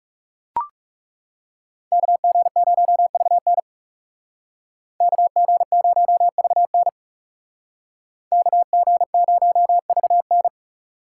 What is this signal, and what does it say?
Morse code, heard as a keyed tone, about 35 wpm, 700 Hz. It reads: KG0VN KG0VN KG0VN
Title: Speed Racing - USA 2x2 Call Signs 30wpm